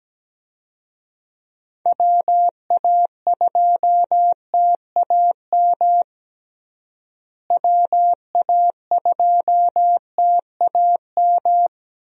Morse code, heard as a keyed tone, about 17 wpm, 700 Hz. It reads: WA2TAM WA2TAM
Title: USA Call Signs - 6 Characters - Repeated 3x - 17wpm